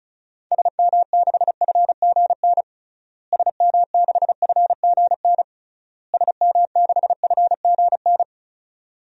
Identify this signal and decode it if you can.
Morse code, heard as a keyed tone, about 36 wpm, 700 Hz. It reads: SM6FGN SM6FGN SM6FGN
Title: DXCC-100 Call Signs - 6 Characters - Repeated 3x - 35wpm